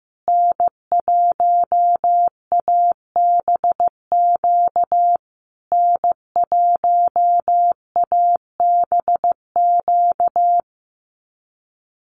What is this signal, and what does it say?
Morse code, heard as a keyed tone, about 15 wpm, 700 Hz. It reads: N1ABQ N1ABQ
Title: USA Call Signs Repeated 3x 15wpm